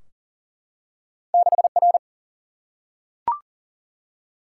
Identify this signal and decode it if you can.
Morse code, heard as a keyed tone, about 38 wpm, 700 Hz. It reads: 6R
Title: Single-Letter Single-Number 40wpm